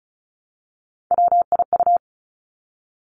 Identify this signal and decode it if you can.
Morse code, heard as a keyed tone, about 36 wpm, 700 Hz. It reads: WIU
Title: Three Challenging Letter-Number Combinations 35wpm